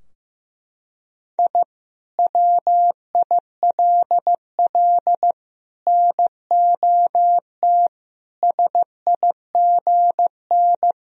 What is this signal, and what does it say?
Morse code, heard as a keyed tone, about 15 wpm, 700 Hz. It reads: I WILL NOT SIGN
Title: Sentences from Top 1000 Words - Review Part 2 - 15wpm